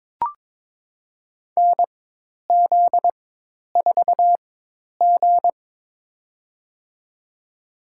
Morse code, heard as a keyed tone, about 22 wpm, spaced as for 10 wpm, 700 Hz. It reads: NZ4G
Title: USA Call Signs - 4 Characters - 4x Character Spacing - 22wpm